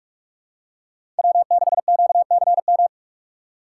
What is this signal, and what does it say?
Morse code, heard as a keyed tone, about 46 wpm, 700 Hz. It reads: W6QCM
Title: USA Call Signs - 5 Characters - 2x Character Spacing - 45wpm